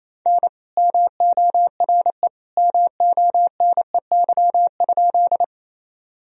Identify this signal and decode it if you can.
Morse code, heard as a keyed tone, about 28 wpm, 700 Hz. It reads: N MORE MONEY?
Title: Sentences from Top 400 Words and Abbreviations 28wpm